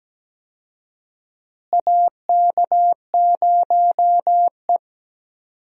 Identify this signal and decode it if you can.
Morse code, heard as a keyed tone, about 17 wpm, 700 Hz. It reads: AK0E
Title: USA Call Signs - 4 Characters - Repeated 3x - 17wpm